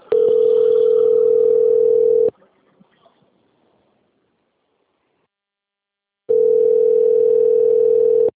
Ringback tone (North America), 440 + 480 Hz.